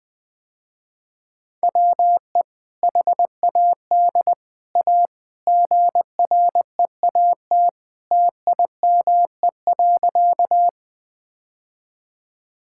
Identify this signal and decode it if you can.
Morse code, heard as a keyed tone, about 19 wpm, 700 Hz. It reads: WE HAD A GREAT TIME.
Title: Sentences with 2-Word Phrases - Spoken Beginning - 20wpm